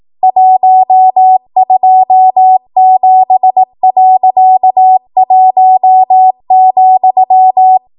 Morse code, heard as a keyed tone, about 18 wpm, 750 Hz. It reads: 127.1,
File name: Archive / 18 WPM / 18WPM